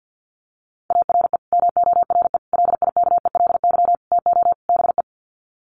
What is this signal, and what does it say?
Morse code, heard as a keyed tone, about 50 wpm, 700 Hz. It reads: ARE MORE LIKELY TO BE
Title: Sets of 5 Words 50wpm